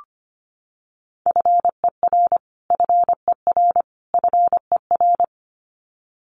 Morse code, heard as a keyed tone, about 25 wpm, 700 Hz. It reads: FER FER FER
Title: Abbreviations Repeated 3x - 25wpm